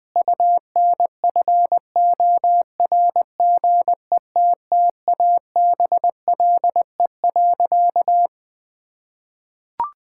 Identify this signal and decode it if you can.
Morse code, heard as a keyed tone, about 21 wpm, 700 Hz. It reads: UNFORGETTABLE.
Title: Sentences with 6-Word Phrases - 20wpm